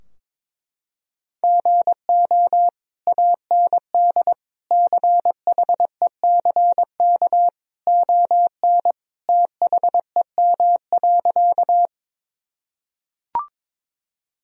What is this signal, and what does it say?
Morse code, heard as a keyed tone, about 21 wpm, 700 Hz. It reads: GO AND CHECK ON THEM.
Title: Sentences from Top 1000 Words - Review Part 3 - 22wpm